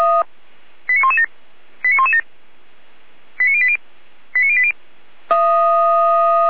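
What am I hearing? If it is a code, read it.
Five-tone selective call (ZVEI), heard as quick runs of five tones, tones of about 70 ms. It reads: 89117 89117 89080 89080
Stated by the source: ZVEI five-tone selective call received on SDR.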